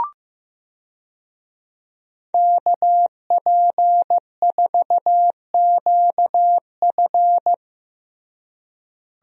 Morse code, heard as a keyed tone, about 15 wpm, 700 Hz. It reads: KP4QF
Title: DXCC-100 Call Signs - 5 Characters - Repeated 2x - 15wpm